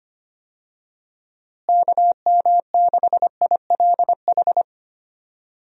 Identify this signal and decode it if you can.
Morse code, heard as a keyed tone, about 25 wpm, 700 Hz. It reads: KM6ILH
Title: USA Call Signs - 6 Characters - Repeated 2x - 25wpm